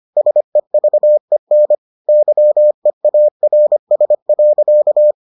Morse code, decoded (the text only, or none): SEVEN YEARS.